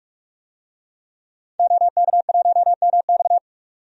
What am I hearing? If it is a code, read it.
Morse code, heard as a keyed tone, about 46 wpm, 700 Hz. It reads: OK1MX
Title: DXCC-100 Call Signs - 5 Characters - 45wpm